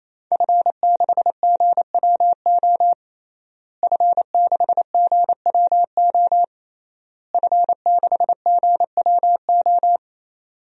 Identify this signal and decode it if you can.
Morse code, heard as a keyed tone, about 28 wpm, 700 Hz. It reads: F6GWO F6GWO F6GWO
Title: DXCC-100 Call Signs - 5 Characters - Repeated 3x - 28wpm